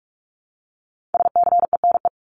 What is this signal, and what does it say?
Morse code, heard as a keyed tone, about 44 wpm, 700 Hz. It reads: SCENE
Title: Top 1001-1250 Words 45wpm